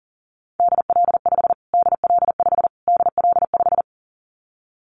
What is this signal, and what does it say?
Morse code, heard as a keyed tone, about 40 wpm, 700 Hz. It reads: DL5 DL5 DL5
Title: Callsign Prefixes for Top 25 DXCC - Repeated 3x - 40wpm